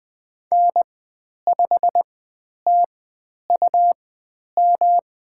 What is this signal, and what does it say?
Morse code, heard as a keyed tone, about 20 wpm, spaced as for 10 wpm, 700 Hz. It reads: N5TUM
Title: 109 Lesson 18g - Letter m - Call Signs 20wpm